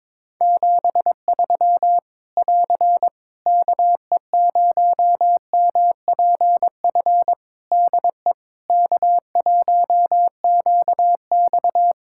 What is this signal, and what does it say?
Morse code, heard as a keyed tone, about 21 wpm, 700 Hz. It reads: 73 <AR> KE0MPF DE K1QX
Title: Practice QSO - Exchange 4 - Ending - 22wpm